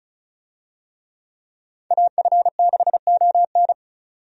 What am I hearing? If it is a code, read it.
Morse code, heard as a keyed tone, about 36 wpm, 700 Hz. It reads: AF6ON
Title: USA Call Signs - 5 Characters - 35wpm